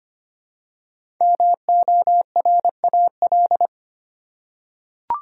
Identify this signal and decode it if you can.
Morse code, heard as a keyed tone, about 25 wpm, 700 Hz. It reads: MORAL
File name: 5 Letter Words - First 3 Letters Given - 25wpm